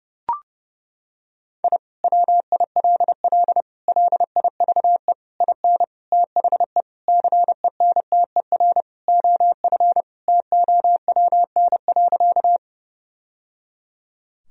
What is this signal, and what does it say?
Morse code, heard as a keyed tone, about 30 wpm, 700 Hz. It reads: I WILL LIVE IN THE CENTER OF TOWN.
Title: Sentences from Top 1000 Words - Review Part 2 - 30wpm